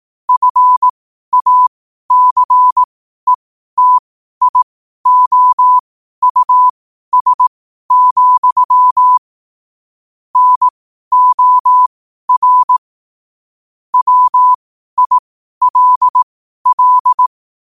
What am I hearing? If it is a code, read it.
Morse code, heard as a keyed tone, about 18 wpm, spaced as for 12 wpm, 1000 Hz. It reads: FACETIOUS, NOR WILL
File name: qotdfiles / 12wpm